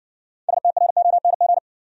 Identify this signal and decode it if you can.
Morse code, heard as a keyed tone, about 60 wpm, 700 Hz. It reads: STRONG